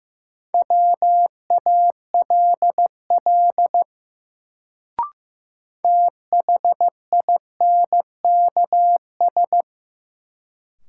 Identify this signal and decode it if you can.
Morse code, heard as a keyed tone, about 15 wpm, 700 Hz. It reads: WALL THINKS
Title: Sentences from Top 1000 Words - Review Part 6 - New Word Drill - 15wpm